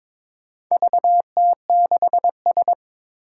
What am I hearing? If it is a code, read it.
Morse code, heard as a keyed tone, about 22 wpm, 700 Hz. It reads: VT6S
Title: Four Challenging Letter-Number Combinations 22wpm